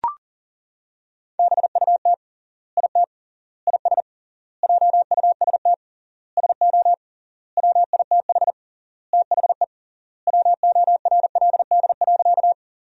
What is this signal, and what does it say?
Morse code, heard as a keyed tone, about 40 wpm, 700 Hz. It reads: BUT IT IS JUST SO WITH THE WORLD.
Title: 3x Word-Spacing - Sentences from Top 200 Words - 40wpm